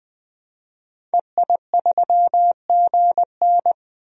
Morse code, heard as a keyed tone, about 20 wpm, 700 Hz. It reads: EI3GN
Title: DXCC-100 Call Signs - 5 Characters - 20wpm